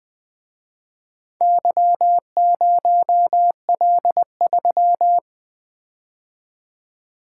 Morse code, heard as a keyed tone, about 24 wpm, 700 Hz. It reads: Y0L3
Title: Speed-Racing - Four Letter-Number Combinations - 20wpm